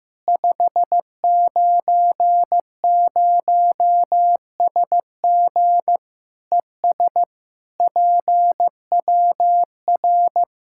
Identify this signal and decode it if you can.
Morse code, heard as a keyed tone, about 15 wpm, 700 Hz. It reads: 590SG ES PWR
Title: K1USN SST QSO Practice 15wpm